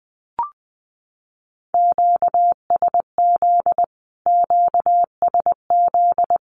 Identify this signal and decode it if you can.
Morse code, heard as a keyed tone, about 20 wpm, 700 Hz. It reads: QSZ QSZ
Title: Q-Codes Repeated 3x 20wpm